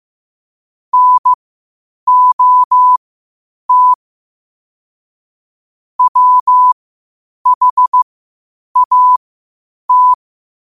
Morse code, heard as a keyed tone, about 15 wpm, spaced as for 8 wpm, 1000 Hz. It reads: NOT WHAT
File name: qotdfiles / 8wpm